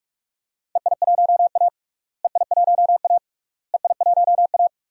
Morse code, heard as a keyed tone, about 46 wpm, 700 Hz. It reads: EI1A EI1A EI1A
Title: DXCC-100 Call Signs - 4 Characters - Repeated 3x - 45wpm